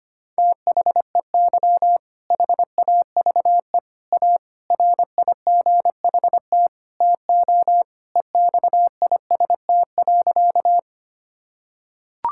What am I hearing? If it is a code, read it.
Morse code, heard as a keyed tone, about 25 wpm, 700 Hz. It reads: THEY HAVE A RIGHT TO EXIST.
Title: Sentences from Top 1000 Words - Review Part 5 - 25wpm